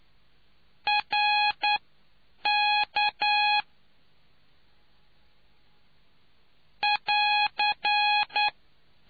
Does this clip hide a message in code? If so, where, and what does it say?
Yes: Morse code from 0.9 to 8.5 s, heard as a keyed tone, about 9 wpm, spaced as for 7 wpm, 830 Hz. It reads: RK <AR>